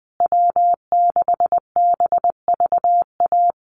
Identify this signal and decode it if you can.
Morse code, heard as a keyed tone, about 20 wpm, 700 Hz. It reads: W6BVA
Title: USA Call Signs Repeated 2x 20wpm